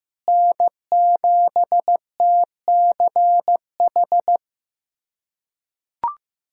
Morse code, heard as a keyed tone, about 16 wpm, 700 Hz. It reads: N7TCH
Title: USA Callsign DE Callsign - 5 Characters - 15wpm